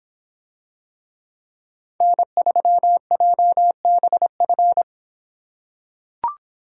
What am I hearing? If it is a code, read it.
Morse code, heard as a keyed tone, about 25 wpm, 700 Hz. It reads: N3JBF